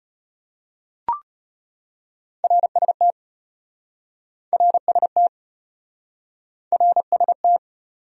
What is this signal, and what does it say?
Morse code, heard as a keyed tone, about 33 wpm, 700 Hz. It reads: RST RST RST